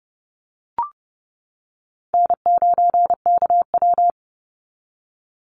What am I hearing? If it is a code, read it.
Morse code, heard as a keyed tone, about 30 wpm, 700 Hz. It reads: N9KW
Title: NA SOTA Chaser Callsigns 30wpm